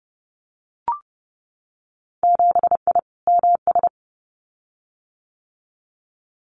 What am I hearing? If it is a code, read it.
Morse code, heard as a keyed tone, about 30 wpm, 700 Hz. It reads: 7I MS
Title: Winter Field Day Exchanges Encore 30wpm